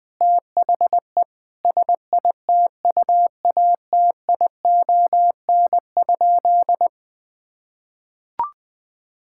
Morse code, heard as a keyed tone, about 20 wpm, 700 Hz. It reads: THE SITUATION?
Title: Sentences with 8-Word Phrases - Spoken Beginning - 20wpm